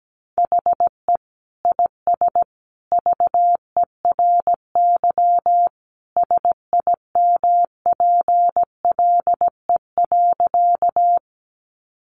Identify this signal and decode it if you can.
Morse code, heard as a keyed tone, about 17 wpm, 700 Hz. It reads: HE IS VERY SIMPLE.
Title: Sentences from Top 400 Words 17wpm